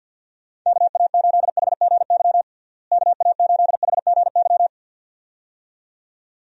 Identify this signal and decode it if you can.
Morse code, heard as a keyed tone, about 52 wpm, 700 Hz. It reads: KA8HGY KA8HGY
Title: USA Call Signs - 6 Characters - Repeated 2x - 50wpm